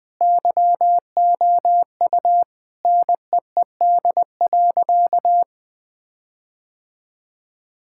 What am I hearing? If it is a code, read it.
Morse code, heard as a keyed tone, about 21 wpm, 700 Hz. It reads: YOU NEED.
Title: Sentences with 5-Word Phrases - Spoken Beginning - 20wpm